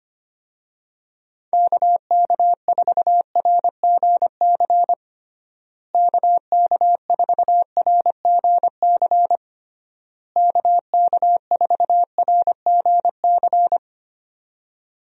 Morse code, heard as a keyed tone, about 25 wpm, 700 Hz. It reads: KK4RGC KK4RGC KK4RGC
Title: USA Call Signs - 6 Characters - Repeated 3x - 25wpm